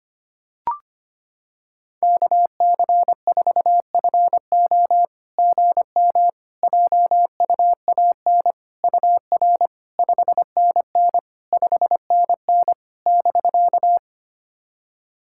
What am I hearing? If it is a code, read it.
Morse code, heard as a keyed tone, about 25 wpm, 700 Hz. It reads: KC4FO GM JUAN UR 5NN 5NN <BK>
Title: SOTA QSO Practice 25wpm